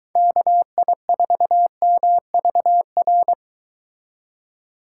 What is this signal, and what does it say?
Morse code, heard as a keyed tone, about 23 wpm, 700 Hz. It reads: KI4MVR